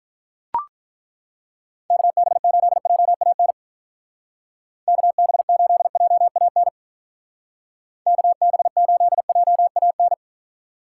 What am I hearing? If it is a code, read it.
Morse code, heard as a keyed tone, about 42 wpm, 700 Hz. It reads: KB8JAN KB8JAN KB8JAN
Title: Speed Racing - USA 2x3 Call Signs 35wpm